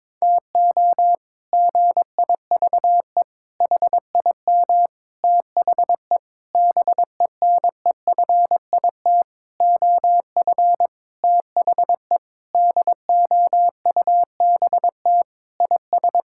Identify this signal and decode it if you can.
Morse code, heard as a keyed tone, about 22 wpm, 700 Hz. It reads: TO GIVE HIM THE BENEFIT OF THE DOUBT IS
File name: practice / 22wpm / Sentences with 8-Word Phrases - 22wpm